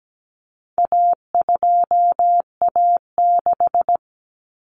Morse code, heard as a keyed tone, about 17 wpm, 700 Hz. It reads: A2A6
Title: Common Cut Numbers - 4-Digit 17wpm